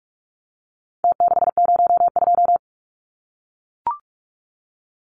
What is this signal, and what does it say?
Morse code, heard as a keyed tone, about 44 wpm, 700 Hz. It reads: T602
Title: Numbers - Contest Style - 4-Digit 45wpm